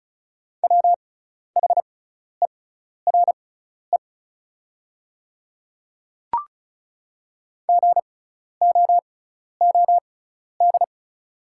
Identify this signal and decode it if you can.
Morse code, heard as a keyed tone, about 36 wpm, spaced as for 12 wpm, 700 Hz. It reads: WHERE GOOD
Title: 6x Character Spacing - Top 200 Words - 35wpm